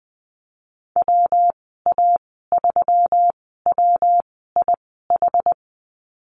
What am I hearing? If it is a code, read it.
Morse code, heard as a keyed tone, about 20 wpm, spaced as for 15 wpm, 700 Hz. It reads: WA3WIH
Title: 122 Lesson 20g - Number 3 - Call Signs 20wpm